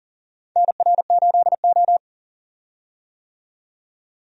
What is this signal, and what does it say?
Morse code, heard as a keyed tone, about 40 wpm, 700 Hz. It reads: NR8O
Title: USA Call Signs - 4 Characters - 40wpm